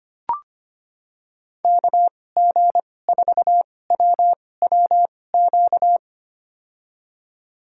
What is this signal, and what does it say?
Morse code, heard as a keyed tone, about 25 wpm, spaced as for 18 wpm, 700 Hz. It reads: KG4WWQ
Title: USA Call Signs - 6 Characters - 2x Character Spacing - 25wpm